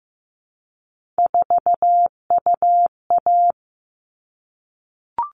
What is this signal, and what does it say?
Morse code, heard as a keyed tone, about 15 wpm, 700 Hz. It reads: 4UA
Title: All Cut Numbers - 3-Digit 15wpm